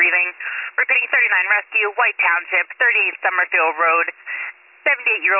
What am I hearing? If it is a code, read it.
Emergency Alert System (SAME) tones (header bursts), heard as harsh buzzing bursts, with speech. It is heard from 0.4 to 4.6 s.